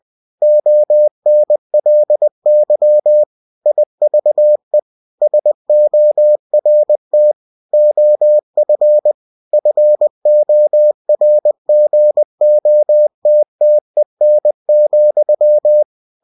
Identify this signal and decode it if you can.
Morse code, heard as a keyed tone, about 20 wpm, 600 Hz. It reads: ONLY IVE SORT OF FORGOTTEN,
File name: cw-books / winnie-the-pooh / winnie-20wpm-page-0084